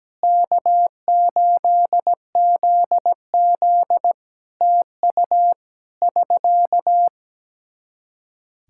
Morse code, heard as a keyed tone, about 17 wpm, 700 Hz. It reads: K8ZZ TU <SK>
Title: K1USN SST QSO Practice 17wpm